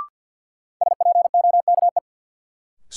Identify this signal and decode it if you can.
Morse code, heard as a keyed tone, about 50 wpm, 700 Hz. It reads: SPOKE
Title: Sentences from Top 1200 Words - New Word Drill - 50wpm